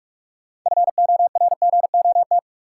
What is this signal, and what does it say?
Morse code, heard as a keyed tone, about 46 wpm, 700 Hz. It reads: FORGOT
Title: Sentences from Top 1400 Words - New Word Drill - 45wpm